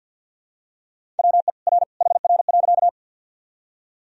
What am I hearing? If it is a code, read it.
Morse code, heard as a keyed tone, about 51 wpm, 700 Hz. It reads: WE R HR.
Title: Sentences from Top 200 Words and Abbreviations 50wpm